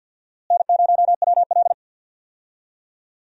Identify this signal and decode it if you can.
Morse code, heard as a keyed tone, about 50 wpm, 700 Hz. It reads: N0WL
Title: USA Call Signs - 4 Characters - Repeated 2x - 50wpm